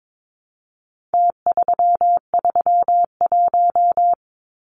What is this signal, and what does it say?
Morse code, heard as a keyed tone, about 22 wpm, 700 Hz. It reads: T331
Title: Numbers - Contest Style - 4-Digit 22wpm